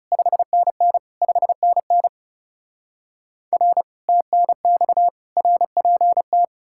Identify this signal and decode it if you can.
Morse code, heard as a keyed tone, about 40 wpm, 700 Hz. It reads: 5NN 5NN R TNX RPT